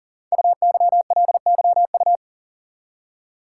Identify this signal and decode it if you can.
Morse code, heard as a keyed tone, about 43 wpm, 700 Hz. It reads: UYLYU
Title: Code Groups - 5 Letters - 40wpm